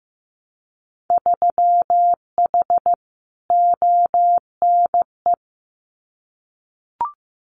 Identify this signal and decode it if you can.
Morse code, heard as a keyed tone, about 15 wpm, 700 Hz. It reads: 3H ONE
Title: Winter Field Day Exchanges Encore 15wpm